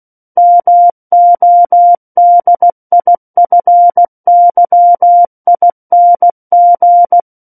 Morse code, heard as a keyed tone, about 16 wpm, 700 Hz. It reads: MODIFYING
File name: Book One / 07-Chapter 7 - 16 WPM